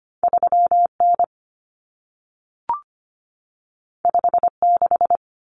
Morse code, heard as a keyed tone, about 25 wpm, 700 Hz. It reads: 3N 56
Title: Common Cut Numbers - 2-Digit 25wpm